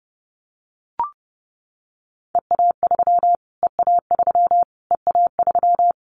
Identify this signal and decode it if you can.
Morse code, heard as a keyed tone, about 30 wpm, 700 Hz. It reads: EA3 EA3 EA3
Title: Callsign Prefixes for Top 25 DXCC - Repeated 3x - 30wpm